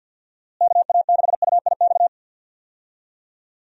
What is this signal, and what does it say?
Morse code, heard as a keyed tone, about 47 wpm, 700 Hz. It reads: KA6UIX
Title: USA Call Signs - 6 Characters - Repeated 2x - 50wpm